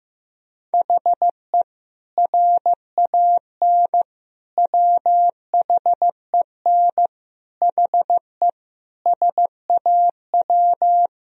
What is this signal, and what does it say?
Morse code, heard as a keyed tone, about 15 wpm, 700 Hz. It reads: HE RAN WHEN HE SAW
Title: Sentences from Top 500 Words and Abbreviations 15wpm